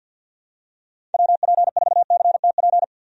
Morse code, heard as a keyed tone, about 50 wpm, 700 Hz. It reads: WP4CTP
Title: USA Call Signs - 6 Characters - 50wpm